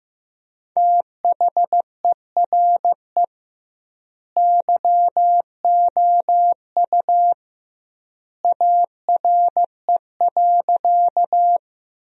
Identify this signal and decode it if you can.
Morse code, heard as a keyed tone, about 15 wpm, 700 Hz. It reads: THERE YOU ARE.
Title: 2x Word-Spacing - Sentences from Top 100 Words - 15wpm